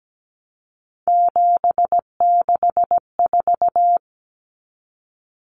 Morse code, heard as a keyed tone, about 17 wpm, 700 Hz. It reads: 764
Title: Numbers - 3-Digit 17wpm